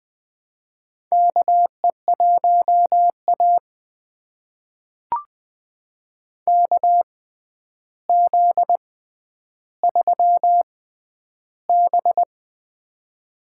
Morse code, heard as a keyed tone, about 20 wpm, spaced as for 10 wpm, 700 Hz. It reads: KE1A KZ3B
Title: USA Call Signs - 4 Characters - 6x Character Spacing - 20wpm